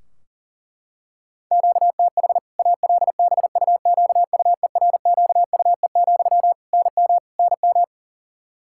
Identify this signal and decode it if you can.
Morse code, heard as a keyed tone, about 40 wpm, 700 Hz. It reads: QTH ALBUQUERQUE, NM NM
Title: K1USN SST QSO Practice 40wpm